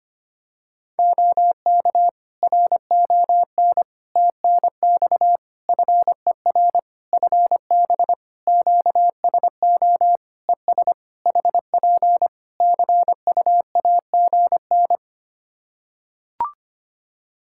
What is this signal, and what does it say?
Morse code, heard as a keyed tone, about 25 wpm, 700 Hz. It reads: OK RON TNX FER FB QSO ES HP CUAGN
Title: Practice QSO - Exchange 4 - Ending - 25wpm